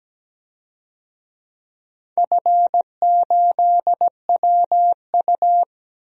Morse code, heard as a keyed tone, about 17 wpm, 700 Hz. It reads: F8WU
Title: DXCC-100 Call Signs - 4 Characters - Repeated 3x - 17wpm